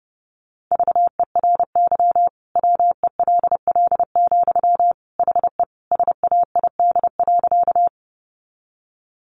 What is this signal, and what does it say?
Morse code, heard as a keyed tone, about 30 wpm, 700 Hz. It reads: VERY WELL, HE SAID.
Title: Sentences from Top 200 Words 30wpm